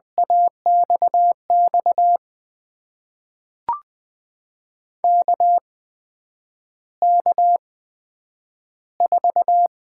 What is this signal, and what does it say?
Morse code, heard as a keyed tone, about 20 wpm, spaced as for 9 wpm, 700 Hz. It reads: AXX KK4
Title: USA Call Signs - 6 Characters - 8x Character Spacing - 20wpm